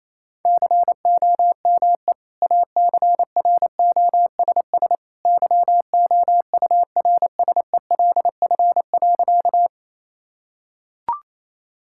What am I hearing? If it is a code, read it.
Morse code, heard as a keyed tone, about 28 wpm, 700 Hz. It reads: COME ACROSS YOURSELF.
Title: Sentences from Top 1000 Words - Review Part 5 - 28wpm